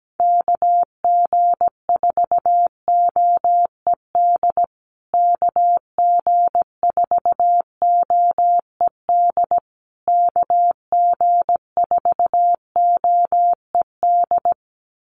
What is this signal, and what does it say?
Morse code, heard as a keyed tone, about 17 wpm, 700 Hz. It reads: KG4OED KG4OED KG4OED
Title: USA Call Signs Repeated 3x 17wpm